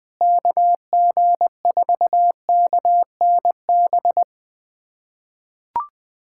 Morse code, heard as a keyed tone, about 20 wpm, 700 Hz. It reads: KG4KNB
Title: USA Callsign DE Callsign - 6 Characters - 20wpm